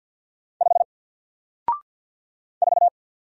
Morse code, heard as a keyed tone, about 50 wpm, 700 Hz. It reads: 5 4
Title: Single Character (Letter:Number:Punctuation:Prosign) Alt 50wpm